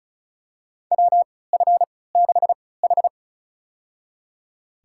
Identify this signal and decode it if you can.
Morse code, heard as a keyed tone, about 34 wpm, spaced as for 20 wpm, 700 Hz. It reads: WF6H
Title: USA Call Signs - 3x Character Spacing - 35wpm